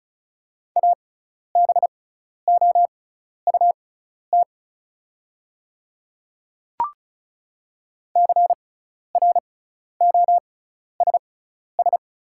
Morse code, heard as a keyed tone, about 36 wpm, spaced as for 12 wpm, 690 Hz. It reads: ABOUT CROSS
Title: 6x Character Spacing - Top 300 Words - 35wpm